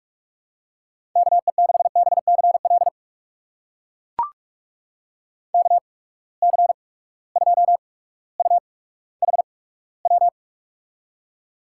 Morse code, heard as a keyed tone, about 44 wpm, spaced as for 19 wpm, 700 Hz. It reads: KE6BCL KC2UHW
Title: USA Call Signs - 6 Characters - 8x Character Spacing - 45wpm